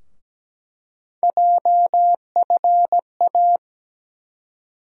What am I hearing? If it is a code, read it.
Morse code, heard as a keyed tone, about 17 wpm, 700 Hz. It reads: JFA
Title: USA Call Sign Postfixes - 3 Letters 17wpm